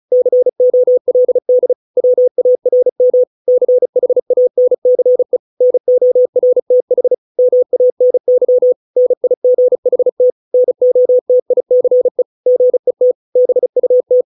Morse code, heard as a keyed tone, about 35 wpm, 500 Hz. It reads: COLD WARM CHANCE NORTH MANY NIGHT NOTICE GET BUT